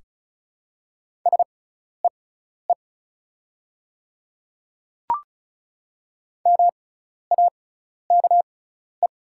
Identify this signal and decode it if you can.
Morse code, heard as a keyed tone, about 36 wpm, spaced as for 12 wpm, 700 Hz. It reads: SEE MAKE
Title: 6x Character Spacing - Top 500 Words - 35wpm